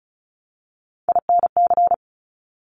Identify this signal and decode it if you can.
Morse code, heard as a keyed tone, about 36 wpm, 700 Hz. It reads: INC
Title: Top Three Letter Combinations in Words 35wpm